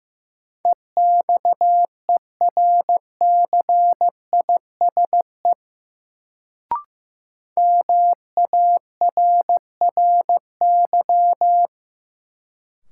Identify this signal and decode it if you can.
Morse code, heard as a keyed tone, about 15 wpm, 700 Hz. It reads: EXERCISE MARRY
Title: Sentences from Top 1200 Words - New Word Drill Alt - 15wpm